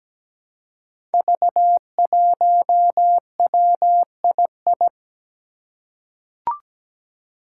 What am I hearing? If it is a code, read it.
Morse code, heard as a keyed tone, about 17 wpm, 700 Hz. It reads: V1WII
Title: Five Challenging Letter-Number Combinations 17wpm